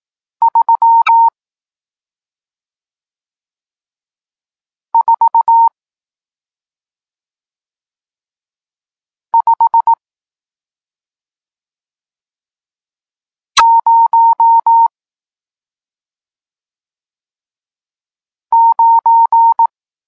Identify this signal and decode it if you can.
Morse code, heard as a keyed tone, about 18 wpm, 920 Hz. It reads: 3 4 5 0 9
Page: Tekst 40 – Nauka znaków przestankowych